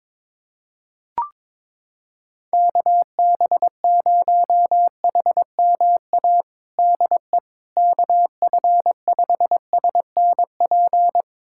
Morse code, heard as a keyed tone, about 22 wpm, 700 Hz. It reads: KB0HMA DE KF5SNP
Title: USA Callsign DE Callsign - 6 Characters - 22wpm